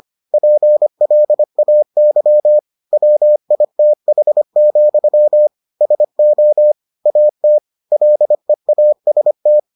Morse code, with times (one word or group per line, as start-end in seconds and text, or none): 0.3-2.6 PLAY
2.9-5.5 WITH,
5.8-6.7 SO
7.1-7.6 AT
7.9-9.6 LEAST